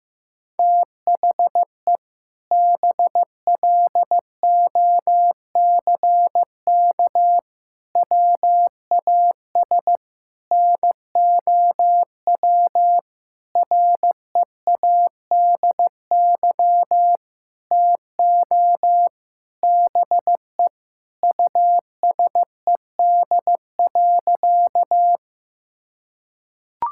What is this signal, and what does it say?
Morse code, heard as a keyed tone, about 15 wpm, 700 Hz. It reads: THE BLOCK WAS NOW READY TO BE USED.
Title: Sentences from Top 1600 Words 15wpm